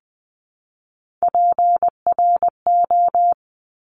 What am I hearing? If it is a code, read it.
Morse code, heard as a keyed tone, about 20 wpm, 700 Hz. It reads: PRO
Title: 260 Lesson 42e - Words 20-20wpm